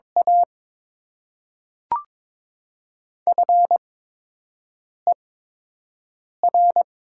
Morse code, heard as a keyed tone, about 22 wpm, spaced as for 6 wpm, 700 Hz. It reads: A FER
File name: Top 20 QSO Elements - 8x Character Spacing - 22wpm